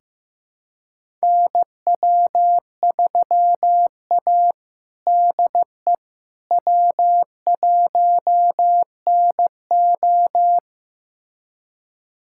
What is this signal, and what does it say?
Morse code, heard as a keyed tone, about 15 wpm, 700 Hz. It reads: NW3A DE W1NO
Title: USA Callsign DE Callsign - 4 Characters - 15wpm